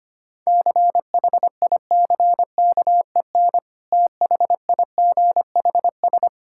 Morse code, heard as a keyed tone, about 25 wpm, 700 Hz. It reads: CHICKEN THIGHS
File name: practice / 25wpm / Difficult Words with Hints - 25wpm